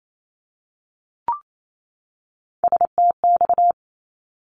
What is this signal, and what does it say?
Morse code, heard as a keyed tone, about 28 wpm, 700 Hz. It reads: STX
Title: ARRL Sections 28wpm